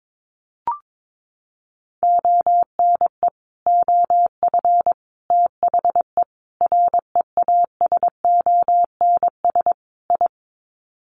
Morse code, heard as a keyed tone, about 22 wpm, 700 Hz. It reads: ONE OF THE REASONS I
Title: Sets of 5 Words Encore2 22wpm